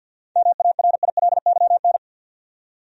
Morse code, heard as a keyed tone, about 48 wpm, 700 Hz. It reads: MARILYN
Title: USA Common Names - 50wpm